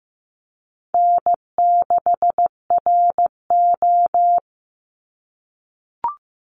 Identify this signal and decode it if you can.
Morse code, heard as a keyed tone, about 15 wpm, 700 Hz. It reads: N6RO
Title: Easy Call Signs - 15wpm